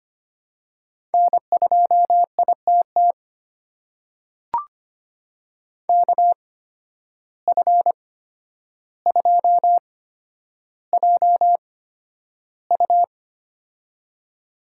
Morse code, heard as a keyed tone, about 25 wpm, spaced as for 11 wpm, 700 Hz. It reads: N2ITT KF2JU
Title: USA Call Signs - 5 Characters - 8x Character Spacing - 25wpm